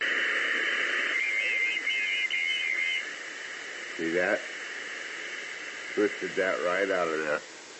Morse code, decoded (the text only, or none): none